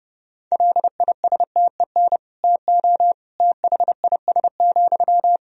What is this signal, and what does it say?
Morse code, heard as a keyed tone, about 30 wpm, 700 Hz. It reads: LISTEN TO THIS,
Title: Sentences from Top 400 Words and Abbreviations 30wpm